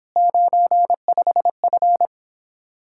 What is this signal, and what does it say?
Morse code, heard as a keyed tone, about 26 wpm, 700 Hz. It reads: 95F